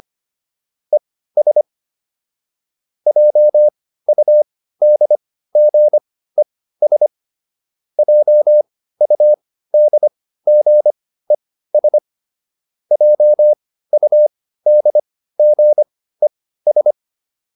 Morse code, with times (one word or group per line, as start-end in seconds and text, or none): 0.9-1.6 ES
3.1-7.1 JUDGES
8.0-12.0 JUDGES
12.9-16.9 JUDGES